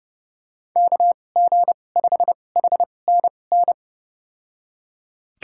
Morse code, heard as a keyed tone, about 29 wpm, spaced as for 21 wpm, 700 Hz. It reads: KG5HNN
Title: USA Call Signs - 6 Characters - 2x Character Spacing - 30wpm